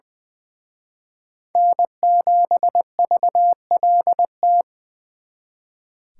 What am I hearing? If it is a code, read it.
Morse code, heard as a keyed tone, about 24 wpm, 700 Hz. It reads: N7VLT
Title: Speed Racing - USA 1x3 Call Signs 20wpm